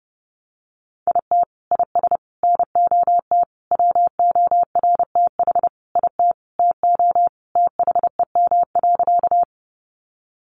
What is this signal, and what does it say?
Morse code, heard as a keyed tone, about 30 wpm, 700 Hz. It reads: IT IS NOT WORTH IT TO THEM.
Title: Sentences from Top 800 Words 30wpm